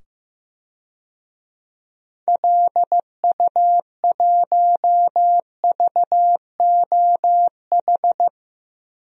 Morse code, heard as a keyed tone, about 15 wpm, 700 Hz. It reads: LU1VOH
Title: DXCC-100 Call Signs - 6 Characters - 15wpm